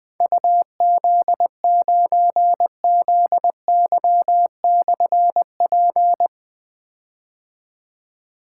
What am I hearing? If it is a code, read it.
Morse code, heard as a keyed tone, about 20 wpm, 700 Hz. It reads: UZ9ZY/P
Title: Difficult Call Signs 20wpm